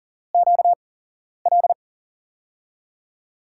Morse code, heard as a keyed tone, about 39 wpm, spaced as for 11 wpm, 700 Hz. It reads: QL
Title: USA Call Signs - 6 Characters - 8x Character Spacing - 40wpm